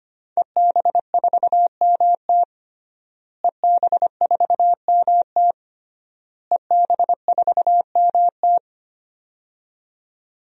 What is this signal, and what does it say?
Morse code, heard as a keyed tone, about 25 wpm, 700 Hz. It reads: EB4MT EB4MT EB4MT
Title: DXCC-100 Call Signs - 5 Characters - Repeated 3x - 25wpm